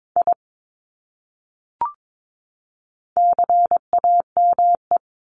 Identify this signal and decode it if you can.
Morse code, heard as a keyed tone, about 22 wpm, 700 Hz. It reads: I CAME
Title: Top 300 Words 22wpm